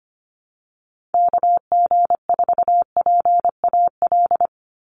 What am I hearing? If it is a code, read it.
Morse code, heard as a keyed tone, about 25 wpm, 700 Hz. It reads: KG4PAL
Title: USA Call Signs 25wpm